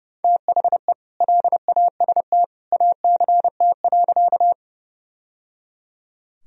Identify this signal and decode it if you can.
Morse code, heard as a keyed tone, about 30 wpm, 700 Hz. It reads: THE LAST ACT.
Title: Sentences from Top 1000 Words - Review Part 1 - 30wpm